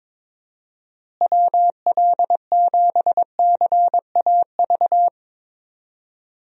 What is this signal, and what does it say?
Morse code, heard as a keyed tone, about 22 wpm, 700 Hz. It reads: WL7CAV
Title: USA Call Signs - 6 Characters - Repeated 3x - 22wpm